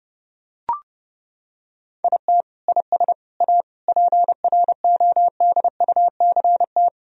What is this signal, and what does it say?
Morse code, heard as a keyed tone, about 30 wpm, 700 Hz. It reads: IT IS A PRODUCT
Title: Sentences from Top 1000 Words - Review Part 4 - 30wpm